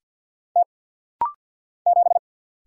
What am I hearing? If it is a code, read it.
Morse code, heard as a keyed tone, about 50 wpm, 700 Hz. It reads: T 7
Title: Single Letter-Number - Rapid-Fire 50wpm